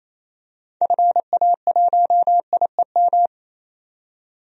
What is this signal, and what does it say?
Morse code, heard as a keyed tone, about 28 wpm, 700 Hz. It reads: FA1IEM
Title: DXCC-100 Call Signs - 6 Characters - 28wpm